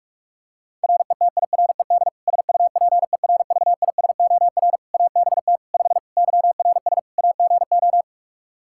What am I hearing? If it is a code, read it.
Morse code, heard as a keyed tone, about 46 wpm, 700 Hz. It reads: RETIRED SUPERVISOR ABT 5 YRS AGO
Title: Practice QSO - Exchange 3 - More Info - 45wpm